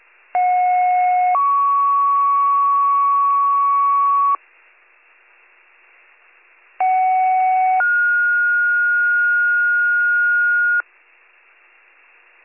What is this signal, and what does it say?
Two-tone page, heard as a one-second tone, then a three-second tone, about 730 then 1120 Hz, about 750 then 1430 Hz. It is heard from 0.3 to 10.8 s.